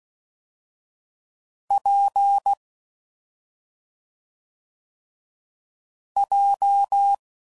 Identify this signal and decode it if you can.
Morse code, heard as a keyed tone, about 16 wpm, 800 Hz. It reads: P J
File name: Combinations / Difficult Combos / AWJ1